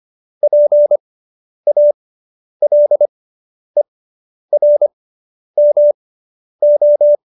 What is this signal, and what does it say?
Morse code, heard as a keyed tone, about 25 wpm, spaced as for 10 wpm, 590 Hz. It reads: PALERMO